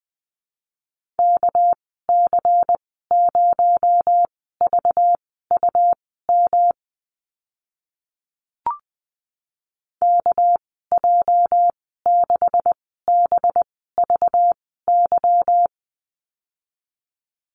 Morse code, heard as a keyed tone, about 20 wpm, spaced as for 14 wpm, 700 Hz. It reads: KC0VUM KJ6BVY
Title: 219 Lesson 35g - Number 0 - Call Signs 20wpm